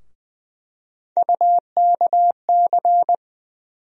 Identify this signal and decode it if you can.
Morse code, heard as a keyed tone, about 20 wpm, 700 Hz. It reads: UKC
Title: USA Call Sign Postfixes - 3 Letters 20wpm